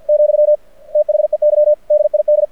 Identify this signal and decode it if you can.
Morse code, heard as a keyed tone, about 51 wpm, 600 Hz. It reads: , THEY DID
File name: archived / CRSnet